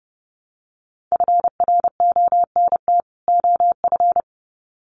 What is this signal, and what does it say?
Morse code, heard as a keyed tone, about 30 wpm, 700 Hz. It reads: FRONT OF
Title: Sets of 2 Words 30wpm